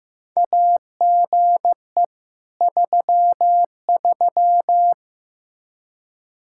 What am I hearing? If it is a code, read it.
Morse code, heard as a keyed tone, about 15 wpm, 700 Hz. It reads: AGE 33
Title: Practice QSO - Exchange 3 - More Info - 15wpm